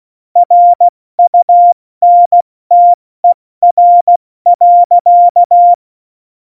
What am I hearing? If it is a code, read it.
Morse code, heard as a keyed tone, about 16 wpm, 700 Hz. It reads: RUNTER.
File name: Morsekurs / Witze / Witz 03.Juli 201, 70 F80